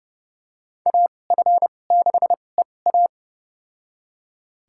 Morse code, heard as a keyed tone, about 31 wpm, spaced as for 22 wpm, 690 Hz. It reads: AF6EA
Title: USA Call Signs - 5 Characters - 2x Character Spacing - 30wpm